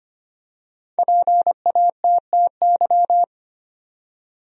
Morse code, heard as a keyed tone, about 25 wpm, 700 Hz. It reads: PATTY
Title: USA Common Names - Max 5 Letters - 25wpm